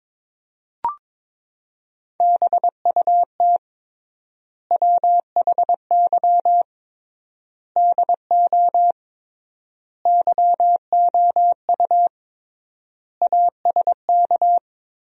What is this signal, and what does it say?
Morse code, heard as a keyed tone, about 22 wpm, 700 Hz. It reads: BUT WHY DO YOU ASK
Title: 3x Word-Spacing - Sentences from Top 200 Words - 22wpm